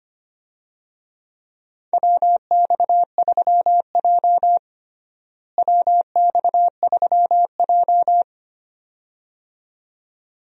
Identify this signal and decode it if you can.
Morse code, heard as a keyed tone, about 25 wpm, 700 Hz. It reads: WX3J WX3J
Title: USA Call Signs - 4 Characters - Repeated 2x - 25wpm